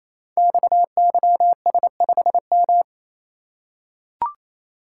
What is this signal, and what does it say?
Morse code, heard as a keyed tone, about 29 wpm, 690 Hz. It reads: XYS5M
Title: Code Groups - 5 Letter:Numbers - 28wpm